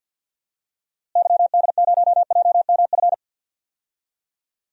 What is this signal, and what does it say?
Morse code, heard as a keyed tone, about 48 wpm, 700 Hz. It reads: YD0JMF
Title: DXCC-100 Call Signs - 6 Characters - Repeated 3x - 50wpm